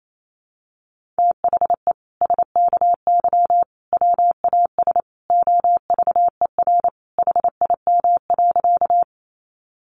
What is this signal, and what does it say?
Morse code, heard as a keyed tone, about 28 wpm, 700 Hz. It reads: THE SKY WAS OVER HIM.
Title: Sentences from Top 1200 Words 28wpm